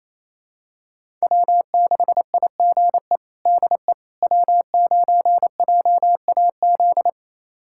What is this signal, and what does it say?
Morse code, heard as a keyed tone, about 28 wpm, 700 Hz. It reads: W6IGE DE W9JAZ
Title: USA Callsign DE Callsign - 5 Characters - 28wpm